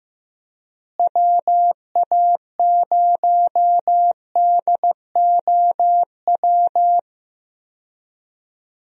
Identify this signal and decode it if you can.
Morse code, heard as a keyed tone, about 15 wpm, 700 Hz. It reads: WA0DOW
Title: USA Call Signs - 6 Characters - Repeated 2x - 15wpm